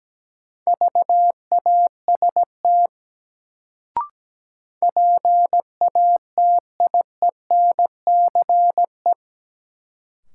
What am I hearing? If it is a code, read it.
Morse code, heard as a keyed tone, about 17 wpm, 700 Hz. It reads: VAST PATIENCE
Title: Sentences from Top 1300 Words - New Word Drill Alt - 17wpm